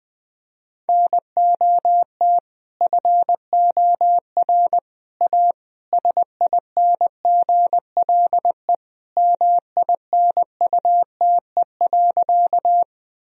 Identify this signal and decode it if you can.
Morse code, heard as a keyed tone, about 21 wpm, 700 Hz. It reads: NOT FOR A SINGLE MINUTE.
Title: Sentences from Top 1000 Words - Review Part 4 - 20wpm